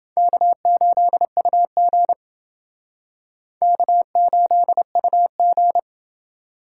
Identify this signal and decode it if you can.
Morse code, heard as a keyed tone, about 28 wpm, 700 Hz. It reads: K8UG K8UG